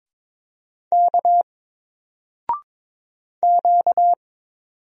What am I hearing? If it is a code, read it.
Morse code, heard as a keyed tone, about 22 wpm, 700 Hz. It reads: K Q
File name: Single-Letter-Number-22wpm